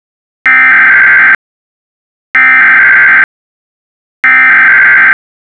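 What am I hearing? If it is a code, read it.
Emergency Alert System (SAME) tones (header bursts), heard as harsh buzzing bursts. It is heard from 0.5 to 5.1 s.
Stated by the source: Generated EAS SAME message from a Python encoder script.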